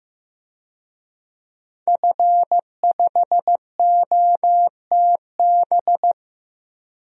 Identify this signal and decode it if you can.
Morse code, heard as a keyed tone, about 15 wpm, 700 Hz. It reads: F5OTB
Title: DXCC-100 Call Signs - 5 Characters - Repeated 3x - 15wpm